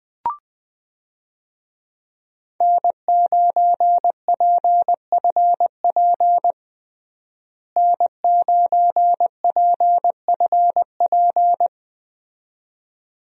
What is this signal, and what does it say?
Morse code, heard as a keyed tone, about 20 wpm, 700 Hz. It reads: N9PFP N9PFP
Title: USA Call Signs - 5 Characters - Repeated 2x - 20wpm